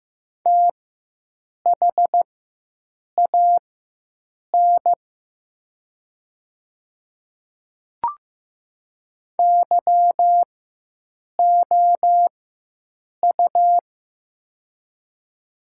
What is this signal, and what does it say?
Morse code, heard as a keyed tone, about 15 wpm, spaced as for 7 wpm, 700 Hz. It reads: THAN YOU
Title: 4x Character Spacing - Top 300 Words - 15wpm